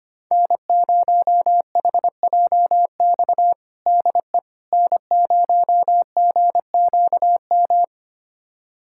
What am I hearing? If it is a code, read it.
Morse code, heard as a keyed tone, about 25 wpm, 700 Hz. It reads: N0HJX DE N0GQM
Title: USA Callsign DE Callsign - 5 Characters - 25wpm